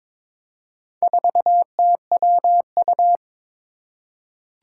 Morse code, heard as a keyed tone, about 22 wpm, 700 Hz. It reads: 4TWU
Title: Four Challenging Letter-Number Combinations 22wpm